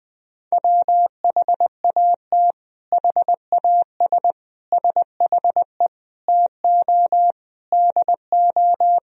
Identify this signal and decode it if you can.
Morse code, heard as a keyed tone, about 21 wpm, 700 Hz. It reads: WHAT HAS SHE TO DO
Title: Sentences from Top 100 Words and Abbreviations 20wpm